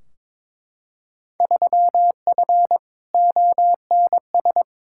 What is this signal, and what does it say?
Morse code, heard as a keyed tone, about 22 wpm, 700 Hz. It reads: 3F ONS
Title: Field Day Exchanges - 22wpm